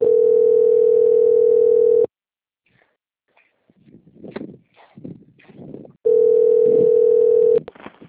Ringback tone (North America), 440 + 480 Hz.